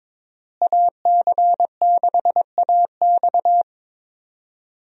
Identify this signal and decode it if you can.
Morse code, heard as a keyed tone, about 22 wpm, 700 Hz. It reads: AC6AX
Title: USA Call Signs - 5 Characters - Repeated 3x - 22wpm